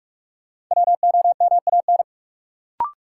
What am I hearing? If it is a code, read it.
Morse code, heard as a keyed tone, about 46 wpm, 700 Hz. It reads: WOMAN
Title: Sentences from Top 1100 Words - New Word Drill Alt - 45wpm